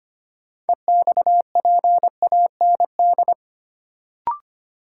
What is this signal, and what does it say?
Morse code, heard as a keyed tone, about 25 wpm, 700 Hz. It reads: EXPAND
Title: Sentences from Top 1400 Words - New Word Drill Alt - 25wpm